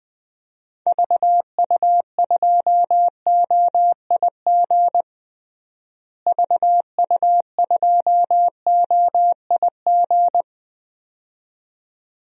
Morse code, heard as a keyed tone, about 21 wpm, 700 Hz. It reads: VU2OIG VU2OIG
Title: DXCC-100 Call Signs - 6 Characters - Repeated 2x - 20wpm